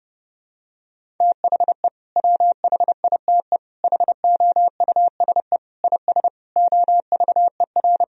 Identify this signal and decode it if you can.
Morse code, heard as a keyed tone, about 30 wpm, 700 Hz. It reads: THE WHITE HOUSE IS OVER
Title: Sentences from Top 1000 Words - Review Part 1 - 30wpm